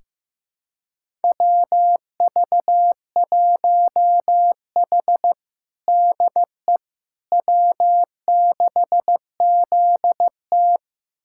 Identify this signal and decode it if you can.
Morse code, heard as a keyed tone, about 16 wpm, 700 Hz. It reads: WV1H DE W6ZT
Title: USA Callsign DE Callsign - 4 Characters - 15wpm